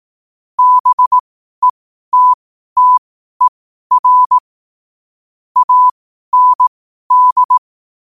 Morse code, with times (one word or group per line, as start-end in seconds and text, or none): 0.6-4.4 BETTER
5.6-7.6 AND